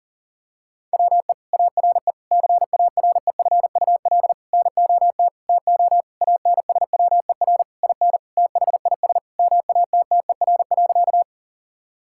Morse code, heard as a keyed tone, about 40 wpm, 700 Hz. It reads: WE ARE CAREFUL NOT TO ANSWER IN THIS MATTER.
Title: Sentences from Top 1600 Words 40wpm